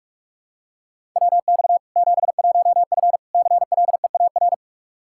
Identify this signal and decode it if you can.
Morse code, heard as a keyed tone, about 42 wpm, 700 Hz. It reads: WX 71F CLEAR
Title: K1USN SST QSO Practice 45wpm